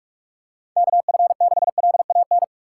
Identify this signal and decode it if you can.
Morse code, heard as a keyed tone, about 45 wpm, 700 Hz. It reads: K F 6 L A N
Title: USA Call Signs - 6 Characters - 6x Character Spacing - 45wpm